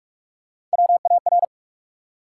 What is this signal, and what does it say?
Morse code, heard as a keyed tone, about 44 wpm, 700 Hz. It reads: WAR
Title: Three Letter Words - 45wpm